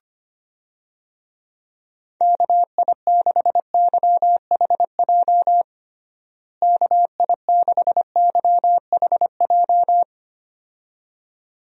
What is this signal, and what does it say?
Morse code, heard as a keyed tone, about 25 wpm, 700 Hz. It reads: KI6YHJ KI6YHJ
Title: USA Call Signs - 6 Characters - Repeated 2x - 25wpm